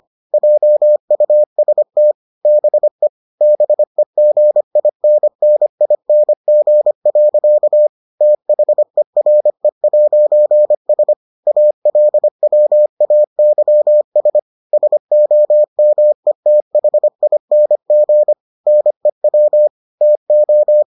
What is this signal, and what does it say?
Morse code, heard as a keyed tone, about 25 wpm, 600 Hz. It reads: JUST BE BEGINNING. THERE'S ALWAYS SOMETHING NEW TO